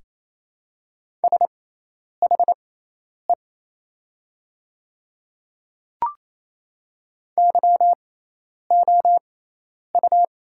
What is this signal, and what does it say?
Morse code, heard as a keyed tone, about 28 wpm, spaced as for 10 wpm, 700 Hz. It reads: SHE YOU
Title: 6x Character Spacing - Top 200 Words - 28wpm